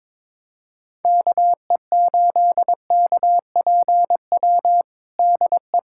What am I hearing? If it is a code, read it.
Morse code, heard as a keyed tone, about 22 wpm, 700 Hz. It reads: KE8KPW DE
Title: USA Callsign DE Callsign 22wpm